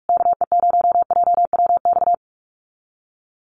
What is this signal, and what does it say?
Morse code, heard as a keyed tone, about 44 wpm, 700 Hz. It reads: KE0JWX
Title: USA Call Signs Repeated 2x 45wpm